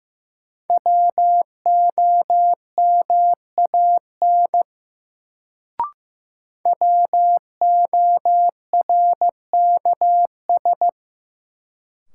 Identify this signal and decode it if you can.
Morse code, heard as a keyed tone, about 15 wpm, 700 Hz. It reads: WOMAN WORKS
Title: Sentences from Top 1100 Words - New Word Drill Alt - 15wpm